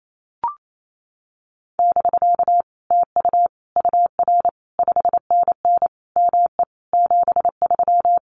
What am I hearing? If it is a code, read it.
Morse code, heard as a keyed tone, about 28 wpm, 700 Hz. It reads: <BK> TU UR 5NN ME 73
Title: Common QSO Phrases Encore 28wpm